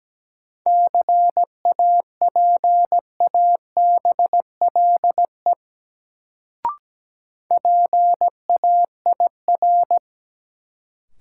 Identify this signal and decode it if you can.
Morse code, heard as a keyed tone, about 17 wpm, 700 Hz. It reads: CAPABLE PAIR
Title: Sentences from Top 1200 Words - New Word Drill Alt - 17wpm